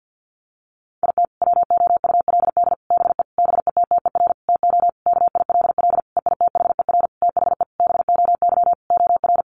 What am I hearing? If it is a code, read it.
Morse code, heard as a keyed tone, about 50 wpm, 700 Hz. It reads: IT WOULD BE BETTER TO KILL EITHER THE BOY OR
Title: Sentences from Top 1500 Words 50wpm